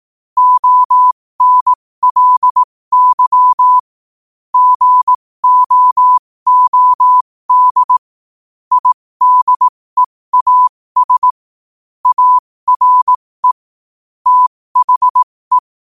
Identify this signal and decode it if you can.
Morse code, heard as a keyed tone, about 18 wpm, spaced as for 15 wpm, 1000 Hz. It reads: ONLY GOOD IDEAS ARE THE
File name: qotdfiles / 15wpm